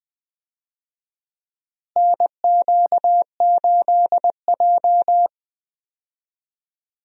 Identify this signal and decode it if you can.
Morse code, heard as a keyed tone, about 20 wpm, 700 Hz. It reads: NQ8J
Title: USA Call Signs - 4 Characters - Repeated 2x - 20wpm